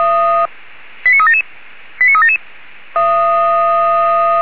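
Five-tone selective call (ZVEI), heard as quick runs of five tones, tones of about 70 ms. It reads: 89377 89377
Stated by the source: ZVEI five-tone selective call received on an SDR.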